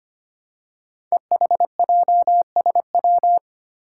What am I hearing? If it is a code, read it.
Morse code, heard as a keyed tone, about 25 wpm, 700 Hz. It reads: EHJSW
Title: Code Groups - 5 Letters - 25wpm